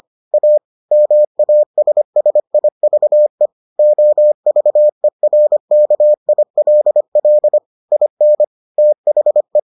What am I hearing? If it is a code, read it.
Morse code, heard as a keyed tone, about 25 wpm, 600 Hz. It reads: A MASSIVE OVERKILL IN THE